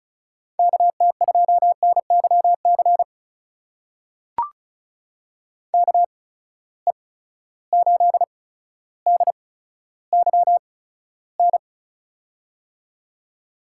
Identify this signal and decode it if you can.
Morse code, heard as a keyed tone, about 34 wpm, spaced as for 15 wpm, 700 Hz. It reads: KT2NYC KE8DYN
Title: USA Call Signs - 6 Characters - 8x Character Spacing - 35wpm